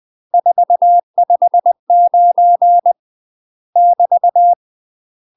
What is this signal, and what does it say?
Morse code, heard as a keyed tone, about 20 wpm, 700 Hz. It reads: 459 <BT>
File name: UK MASTER QSOs TRAINING/20WPM